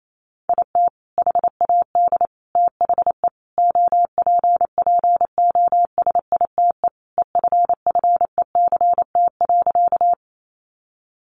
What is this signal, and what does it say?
Morse code, heard as a keyed tone, about 28 wpm, 700 Hz. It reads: IT HAD THE OPPOSITE EFFECT.
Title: Sentences from Top 1300 Words 28wpm